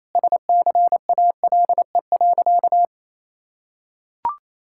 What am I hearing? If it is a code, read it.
Morse code, heard as a keyed tone, about 28 wpm, 700 Hz. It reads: SCALE.
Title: Sentences from Top 1000 Words - Review Part 5 - 28wpm